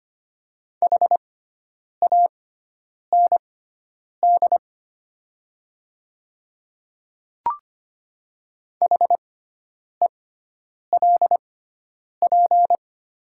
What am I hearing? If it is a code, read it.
Morse code, heard as a keyed tone, about 25 wpm, spaced as for 9 wpm, 700 Hz. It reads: HAND HELP
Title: 6x Character Spacing - Top 200 Words - 25wpm